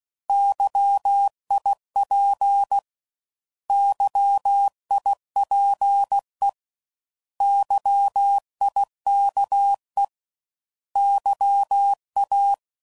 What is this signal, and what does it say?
Morse code, heard as a keyed tone, about 16 wpm, 800 Hz. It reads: YIP YIPE YIKE YA
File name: Prefixes / By First Letter / Xs